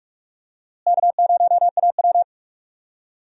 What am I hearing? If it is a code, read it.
Morse code, heard as a keyed tone, about 44 wpm, 700 Hz. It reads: K0AW
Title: USA Call Signs - 4 Characters - Repeated 3x - 45wpm